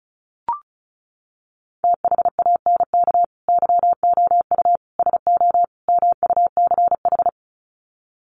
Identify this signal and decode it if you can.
Morse code, heard as a keyed tone, about 34 wpm, 700 Hz. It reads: THANK YOU SO MUCH
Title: Sets of 4 Words 35wpm